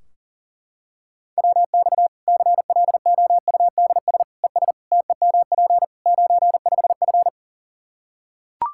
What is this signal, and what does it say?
Morse code, heard as a keyed tone, about 40 wpm, 700 Hz. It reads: WX CLOUDS ES TEMP 95F
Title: Practice QSO - Exchange 2 - Info - 40wpm